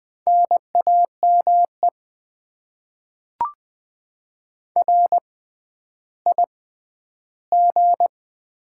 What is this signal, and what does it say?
Morse code, heard as a keyed tone, about 20 wpm, spaced as for 11 wpm, 700 Hz. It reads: NAME RIG
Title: Top 20 QSO Elements - 6x Character Spacing - 20wpm